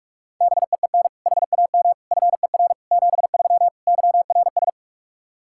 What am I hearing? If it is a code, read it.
Morse code, heard as a keyed tone, about 46 wpm, 700 Hz. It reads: BEEN HAM FER 73 YRS
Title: Practice QSO - Exchange 3 - More Info - 45wpm